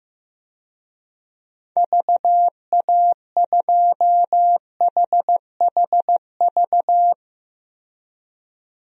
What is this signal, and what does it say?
Morse code, heard as a keyed tone, about 15 wpm, 700 Hz. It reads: VA2HHV
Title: DXCC-100 Call Signs - 6 Characters - 15wpm